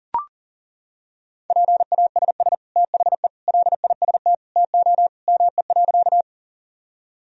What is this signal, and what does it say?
Morse code, heard as a keyed tone, about 40 wpm, 700 Hz. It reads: PASS THE LIST TO ME.
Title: Sentences from Top 1000 Words - Review Part 2 - 40wpm